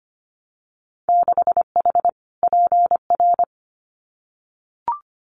Morse code, heard as a keyed tone, about 25 wpm, 700 Hz. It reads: 6H PR
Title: Winter Field Day Exchanges Encore 25wpm